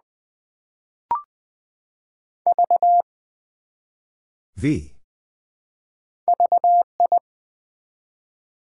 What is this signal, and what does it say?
Morse code, heard as a keyed tone, about 20 wpm, 700 Hz. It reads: V VI
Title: Build A Word - 1 Letter At A Time - 3 to 5-Letter Words - 20wpm